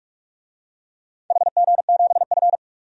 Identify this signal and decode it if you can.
Morse code, heard as a keyed tone, about 44 wpm, 700 Hz. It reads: HG7F
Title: DXCC-100 Call Signs - 4 Characters - 45wpm